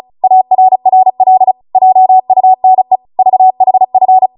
Morse code, heard as a keyed tone, about 36 wpm, 750 Hz. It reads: ARRL JUNE VHF